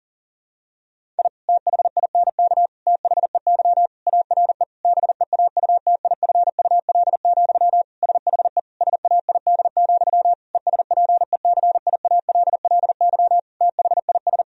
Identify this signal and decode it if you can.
Morse code, heard as a keyed tone, about 40 wpm, 700 Hz. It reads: I THINK THEY ARE BEAUTIFUL, SHE SAID, ESPECIALLY THIS
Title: Sentences from Top 1000 Words - Review Part 4 - 40wpm